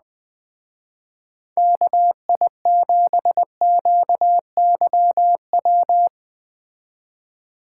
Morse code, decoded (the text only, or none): KI7QYW